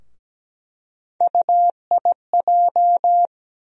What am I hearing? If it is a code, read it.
Morse code, heard as a keyed tone, about 17 wpm, 700 Hz. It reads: UIJ
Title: USA Call Sign Postfixes - 3 Letters 17wpm